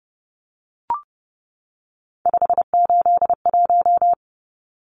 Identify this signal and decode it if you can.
Morse code, heard as a keyed tone, about 30 wpm, 700 Hz. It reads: 581
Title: Numbers - 3-Digit 30wpm